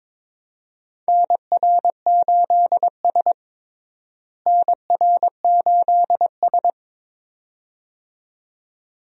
Morse code, heard as a keyed tone, about 22 wpm, 700 Hz. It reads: NR8S NR8S
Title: USA Call Signs - 4 Characters - Repeated 2x - 22wpm